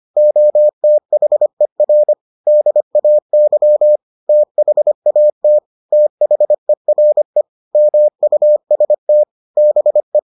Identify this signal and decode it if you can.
Morse code, heard as a keyed tone, about 25 wpm, 600 Hz. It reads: OTHER DAY THAT THERE MUST BE